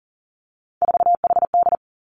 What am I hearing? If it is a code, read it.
Morse code, heard as a keyed tone, about 39 wpm, 700 Hz. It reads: 4HD
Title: Three Challenging Letter-Number Combinations 40wpm